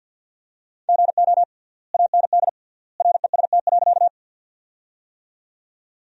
Morse code, heard as a keyed tone, about 53 wpm, 700 Hz. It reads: GO AND REST.
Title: 3x Word-Spacing - Sentences from Top 500 Words - 50wpm